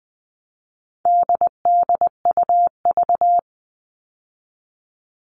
Morse code, heard as a keyed tone, about 20 wpm, 700 Hz. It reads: DDUV
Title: All Cut Numbers - 4-Digit 20wpm